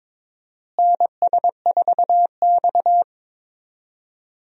Morse code, heard as a keyed tone, about 22 wpm, 700 Hz. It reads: NS4X
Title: USA Call Signs - 4 Characters - Repeated 3x - 22wpm